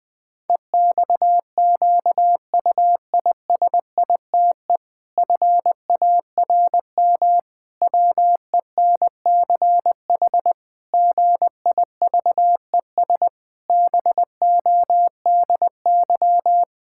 Morse code, heard as a keyed tone, about 21 wpm, 700 Hz. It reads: EXQUISITE FARM WENCH GIVES BODY
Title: Pangram Challenge v2 - 20wpm